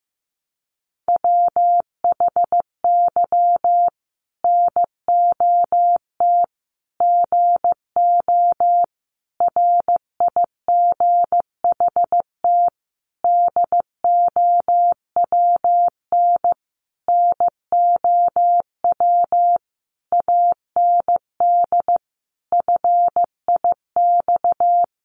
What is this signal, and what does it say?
Morse code, heard as a keyed tone, about 15 wpm, 700 Hz. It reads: WHY NOT GO RIGHT DOWN NOW AND FIX
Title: Sentences from Top 1400 Words 15wpm